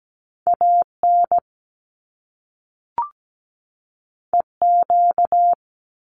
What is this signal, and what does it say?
Morse code, heard as a keyed tone, about 17 wpm, 700 Hz. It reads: AN EQ
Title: Top Two Letter Combinations in Words 17wpm